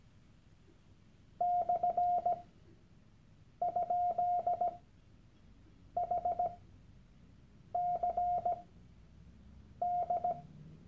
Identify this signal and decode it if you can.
Morse code, heard as a keyed tone, about 17 wpm, spaced as for 6 wpm, 710 Hz. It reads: /?HCD
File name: Workshop Dateien / CW-Online-Kurs / Lektion 001